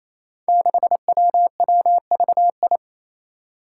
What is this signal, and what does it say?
Morse code, heard as a keyed tone, about 29 wpm, 700 Hz. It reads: 6WWVI
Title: Five Challenging Letter-Number Combinations 28wpm